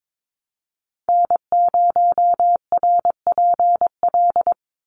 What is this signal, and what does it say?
Morse code, heard as a keyed tone, about 22 wpm, 700 Hz. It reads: N0RPL
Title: USA Call Signs 22wpm